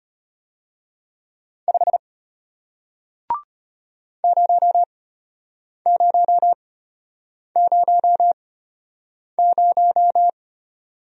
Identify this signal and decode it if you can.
Morse code, heard as a keyed tone, about 33 wpm, spaced as for 8 wpm, 700 Hz. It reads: 5 0000